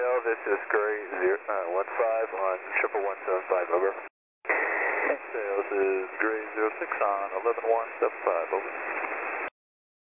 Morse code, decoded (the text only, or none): none